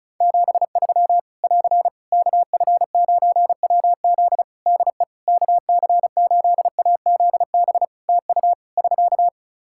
Morse code, heard as a keyed tone, about 34 wpm, 700 Hz. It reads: 73 <AR> KF9WZ DE KC8AZB TU <SK>
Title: Practice QSO - Exchange 4 - Ending - 35wpm